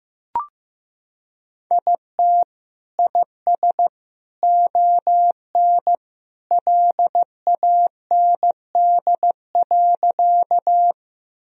Morse code, heard as a keyed tone, about 15 wpm, 700 Hz. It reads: IT IS ON LAND.
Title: Sentences from Top 1000 Words - Review Part 1 - 15wpm